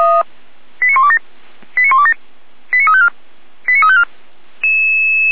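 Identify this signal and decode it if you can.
Five-tone selective call (ZVEI), heard as quick runs of five tones, tones of about 70 ms. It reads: 89127 89127 89354 89354
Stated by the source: ZVEI five-tone selective call 89235, received by SDR, Feuerwehr Orlamünde fire brigade.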